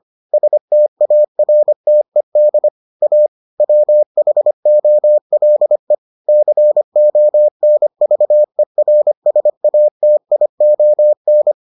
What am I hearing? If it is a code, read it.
Morse code, heard as a keyed tone, about 25 wpm, 600 Hz. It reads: STARTED A WHOLE CONVERSATION